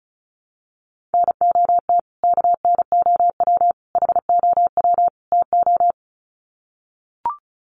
Morse code, heard as a keyed tone, about 34 wpm, 700 Hz. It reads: NOT KNOW HOW TO
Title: Sets of 4 Words 35wpm